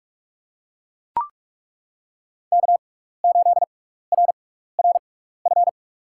Morse code, heard as a keyed tone, about 41 wpm, spaced as for 15 wpm, 700 Hz. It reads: K8RRF
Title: USA Call Signs - 6x Character Spacing - 45wpm